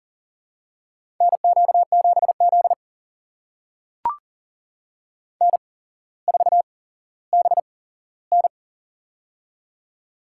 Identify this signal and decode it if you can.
Morse code, heard as a keyed tone, about 40 wpm, spaced as for 17 wpm, 700 Hz. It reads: NQ7Z N4BN
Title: USA Call Signs - 4 Characters - 8x Character Spacing - 40wpm